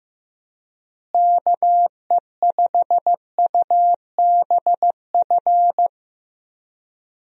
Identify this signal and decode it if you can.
Morse code, heard as a keyed tone, about 15 wpm, 700 Hz. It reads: KE5UBF
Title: USA Call Signs - 6 Characters - Repeated 2x - 15wpm